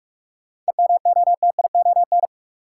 Morse code, heard as a keyed tone, about 47 wpm, 700 Hz. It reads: EMOTION
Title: Sentences from Top 1600 Words - New Word Drill - 45wpm